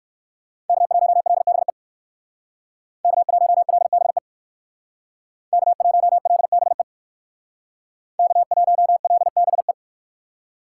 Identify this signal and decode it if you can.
Morse code, heard as a keyed tone, about 59 wpm, 700 Hz. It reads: K1LBE K1LBE K1LBE K1LBE